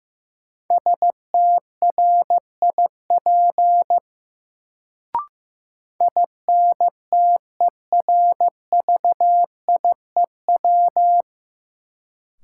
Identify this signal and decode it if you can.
Morse code, heard as a keyed tone, about 15 wpm, 700 Hz. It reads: STRIP INTERVIEW
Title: Sentences from Top 1600 Words - New Word Drill Alt - 15wpm